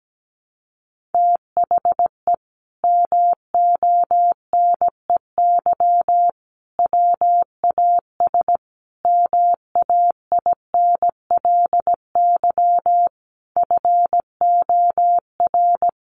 Morse code, heard as a keyed tone, about 17 wpm, 700 Hz. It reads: THE MONEY WAS MAINLY FOR
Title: Sentences from Top 1400 Words 17wpm